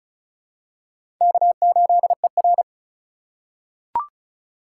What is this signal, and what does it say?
Morse code, heard as a keyed tone, about 36 wpm, 700 Hz. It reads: K8ER
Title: USA Call Signs - 4 Characters - 3x Character Spacing - 35wpm